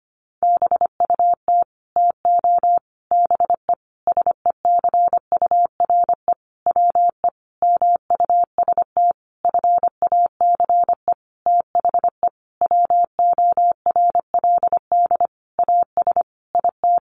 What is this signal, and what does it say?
Morse code, heard as a keyed tone, about 25 wpm, 700 Hz. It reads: BUT TO BE SECURE WE MUST FACE THE WORLD AS IT
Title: Sentences from Top 1200 Words 25wpm